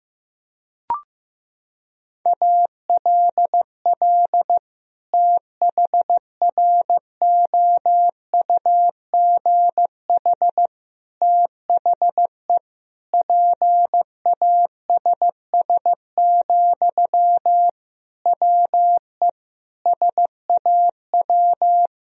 Morse code, heard as a keyed tone, about 15 wpm, 700 Hz. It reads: ALL THROUGH THE PASS, WE SAW
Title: Sentences from Top 1000 Words - Review Part 2 - 15wpm